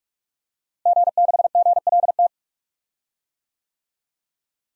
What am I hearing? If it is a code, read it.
Morse code, heard as a keyed tone, about 46 wpm, 700 Hz. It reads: G6GLT
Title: DXCC-100 Call Signs - 5 Characters - Repeated 2x - 45wpm